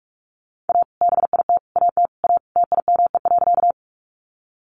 Morse code, heard as a keyed tone, about 44 wpm, 700 Hz. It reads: A BIT AT A TIME.
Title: Sentences from Top 900 Words 45wpm